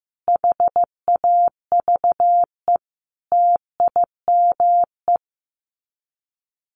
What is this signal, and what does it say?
Morse code, heard as a keyed tone, about 15 wpm, 700 Hz. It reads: HAVE TIME
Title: Sets of 5 Words Encore 15wpm